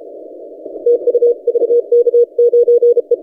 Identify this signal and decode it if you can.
Morse code, heard as a keyed tone, about 33 wpm, 480 Hz. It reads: TU VK9E
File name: posts / vk9ec-40cw- -1410z